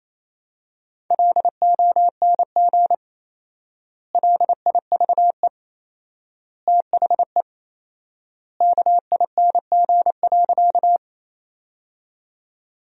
Morse code, heard as a keyed tone, about 28 wpm, 700 Hz. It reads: LONG LIVE THE KING.
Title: 4x Word-Spacing - Sentences from Top 400 Words - 28wpm